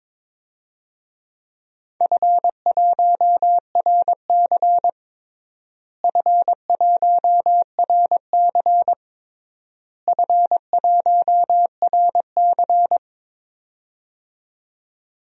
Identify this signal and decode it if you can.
Morse code, heard as a keyed tone, about 22 wpm, 700 Hz. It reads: F1RC F1RC F1RC
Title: DXCC-100 Call Signs - 4 Characters - Repeated 3x - 22wpm